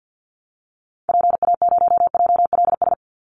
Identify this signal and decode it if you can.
Morse code, heard as a keyed tone, about 50 wpm, 700 Hz. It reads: PA0JLS
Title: Call Signs Repeated 2x - 50wpm